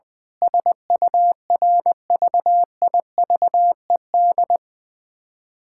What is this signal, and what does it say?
Morse code, heard as a keyed tone, about 20 wpm, 700 Hz. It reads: SURVIVED
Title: Sentences with 7-Word Phrases - 20wpm